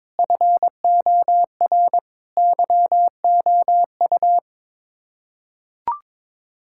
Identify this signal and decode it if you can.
Morse code, heard as a keyed tone, about 22 wpm, 700 Hz. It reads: FOR YOU
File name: Sets of 9 Words - 22wpm